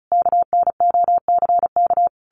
Morse code, heard as a keyed tone, about 34 wpm, 700 Hz. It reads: KNOCK
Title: A Sound Of Thunder 35wpm